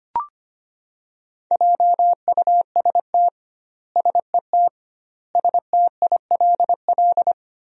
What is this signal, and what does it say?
Morse code, heard as a keyed tone, about 25 wpm, 700 Hz. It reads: JUST SET STILL
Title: 2x Word-Spacing - Sentences from Top 300 Words - 25wpm